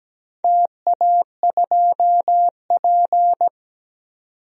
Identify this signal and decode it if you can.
Morse code, heard as a keyed tone, about 17 wpm, 700 Hz. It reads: TA2P
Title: DXCC-100 Call Signs - 4 Characters - 17wpm